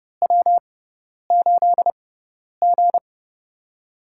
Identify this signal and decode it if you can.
Morse code, heard as a keyed tone, about 30 wpm, spaced as for 10 wpm, 700 Hz. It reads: W8G
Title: USA Call Signs - 4 Characters - 6x Character Spacing - 30wpm